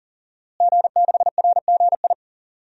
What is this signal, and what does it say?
Morse code, heard as a keyed tone, about 40 wpm, 700 Hz. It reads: G6RGI
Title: DXCC-100 Call Signs - 5 Characters - Repeated 2x - 40wpm